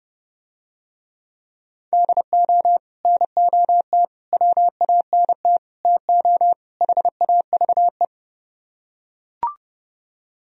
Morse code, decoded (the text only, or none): DO NOT WANT TO HAVE